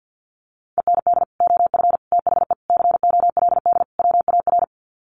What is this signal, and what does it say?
Morse code, heard as a keyed tone, about 50 wpm, 700 Hz. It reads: END OF THE COLD WAR
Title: Sets of 5 Words Encore 50wpm